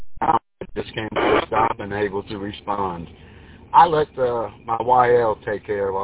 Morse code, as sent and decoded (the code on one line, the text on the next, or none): none